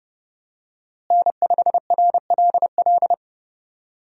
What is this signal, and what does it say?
Morse code, heard as a keyed tone, about 32 wpm, 700 Hz. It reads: N5RLL
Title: USA Call Signs - 5 Characters - Repeated 2x - 30wpm